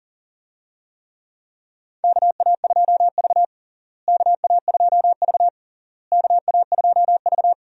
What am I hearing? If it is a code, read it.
Morse code, heard as a keyed tone, about 40 wpm, 700 Hz. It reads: KA2V KA2V KA2V
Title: USA Call Signs - 4 Characters - Repeated 3x - 40wpm